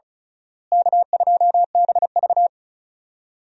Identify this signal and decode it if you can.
Morse code, heard as a keyed tone, about 34 wpm, 700 Hz. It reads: K2BV
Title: USA Call Signs - 4 Characters - Repeated 2x - 35wpm